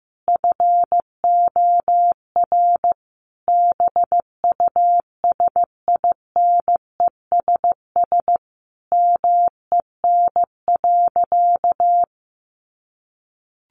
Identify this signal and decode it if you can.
Morse code, heard as a keyed tone, about 15 wpm, 700 Hz. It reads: FOR BUSINESS MEN.
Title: Sentences from Top 1200 Words 15wpm